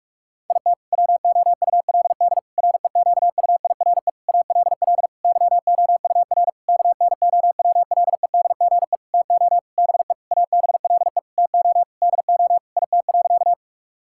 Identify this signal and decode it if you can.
Morse code, heard as a keyed tone, about 48 wpm, 700 Hz. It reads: IT WOULD REQUIRE ALL YOUR KNOWLEDGE TO BE ABLE TO DO IT.
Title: Sentences from Top 1000 Words - Review Part 6 - 45wpm